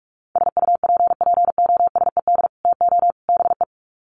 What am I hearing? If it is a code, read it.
Morse code, heard as a keyed tone, about 44 wpm, 700 Hz. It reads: SUPPOSED TO BE
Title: Sets of 3 Words 45wpm